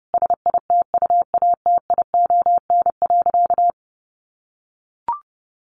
Morse code, heard as a keyed tone, about 30 wpm, 700 Hz. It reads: SITUATION.
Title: Sentences from Top 1200 Words 30wpm